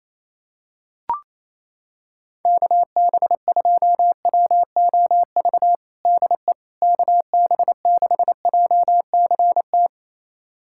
Morse code, heard as a keyed tone, about 28 wpm, 700 Hz. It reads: KB2WOV DE KB6JCT
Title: USA Callsign DE Callsign - 6 Characters - 28wpm